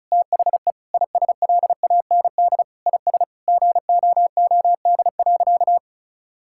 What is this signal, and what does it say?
Morse code, heard as a keyed tone, about 36 wpm, 700 Hz. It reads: THE ISLAND IS GOOD.
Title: Sentences from Top 1000 Words - Review Part 6 - 35wpm